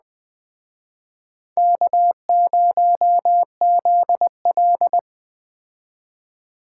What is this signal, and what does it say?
Morse code, heard as a keyed tone, about 20 wpm, 700 Hz. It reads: K0ZL
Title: Speed Racing - USA 1x2 Call Signs 17wpm